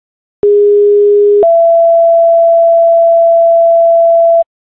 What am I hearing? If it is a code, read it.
Two-tone page, heard as a one-second tone, then a three-second tone, about 410 then 670 Hz. It is heard from 0.4 to 4.4 s.